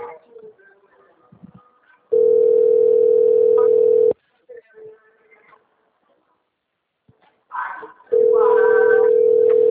Ringback tone (North America), 440 + 480 Hz.
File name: out- - - - - .17097